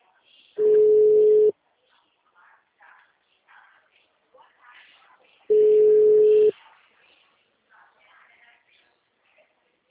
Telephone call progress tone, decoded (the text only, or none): ringback tone